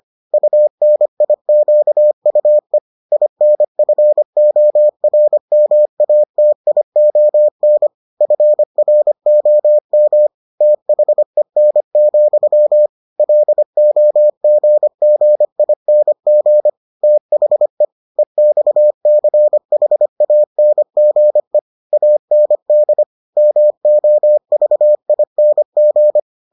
Morse code, heard as a keyed tone, about 25 wpm, 600 Hz. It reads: UNIQUE INFORMATION FROM THEN, LOGGING THE EXCHANGE AND MOVING